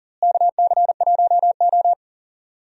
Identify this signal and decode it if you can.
Morse code, heard as a keyed tone, about 40 wpm, 700 Hz. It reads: KC1O
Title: USA Call Signs - 4 Characters - Repeated 2x - 40wpm